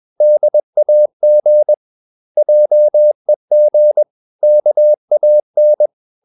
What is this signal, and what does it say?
Morse code, heard as a keyed tone, about 21 wpm, 600 Hz. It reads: DAG JEG KAN
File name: Ferie QRS 21wpm